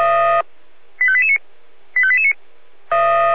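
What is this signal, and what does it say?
Five-tone selective call (ZVEI), heard as quick runs of five tones, tones of about 70 ms. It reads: 85889 85889
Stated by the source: ZVEI five-tone selective call received on an SDR.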